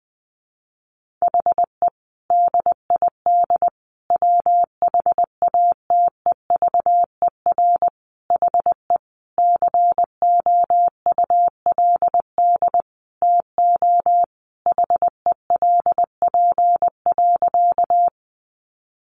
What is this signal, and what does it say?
Morse code, heard as a keyed tone, about 20 wpm, 700 Hz. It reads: HE DID WHATEVER HE COULD TO HELP.
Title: Sentences from Top 1100 Words 20wpm